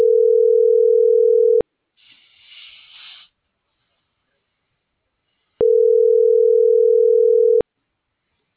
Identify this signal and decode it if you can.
Ringback tone (North America), 440 + 480 Hz.
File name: CENTRAL NAVIERAS / out leonardo.pe.a